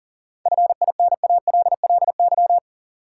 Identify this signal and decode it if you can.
Morse code, heard as a keyed tone, about 40 wpm, 700 Hz. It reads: FINALLY
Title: Sentences from Top 1000 Words - New Word Drill - 40wpm